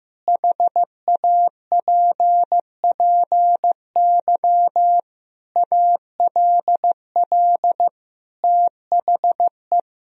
Morse code, heard as a keyed tone, about 15 wpm, 700 Hz. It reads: HAPPY ALL THE
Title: Sentences from Top 1000 Words - Review Part 4 - 15wpm